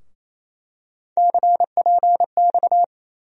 Morse code, heard as a keyed tone, about 28 wpm, 700 Hz. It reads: CPX
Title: USA Call Sign Postfixes - 3 Letters 28wpm